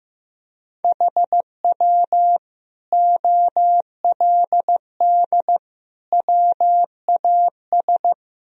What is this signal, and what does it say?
Morse code, heard as a keyed tone, about 15 wpm, 700 Hz. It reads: HW OLD WAS
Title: Sentences from Top 200 Words and Abbreviations 15wpm